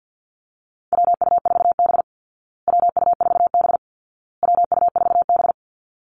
Morse code, heard as a keyed tone, about 50 wpm, 700 Hz. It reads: WU4B WU4B WU4B
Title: Call Signs Repeated 2x - 50wpm